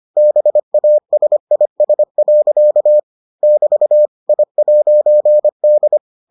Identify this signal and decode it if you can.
Morse code, heard as a keyed tone, about 25 wpm, 600 Hz. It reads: BASIS. <BT> I'D